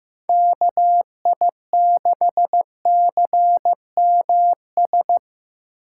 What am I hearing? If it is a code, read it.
Morse code, heard as a keyed tone, about 15 wpm, 700 Hz. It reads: KI6CMS
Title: USA Call Signs - 6x Character Spacing - 15wpm